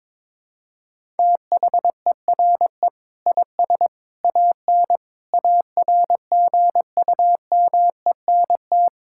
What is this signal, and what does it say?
Morse code, heard as a keyed tone, about 22 wpm, 700 Hz. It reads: THERE IS AN ARGUMENT
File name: practice / 22wpm / Sets of 8 Words - 22wpm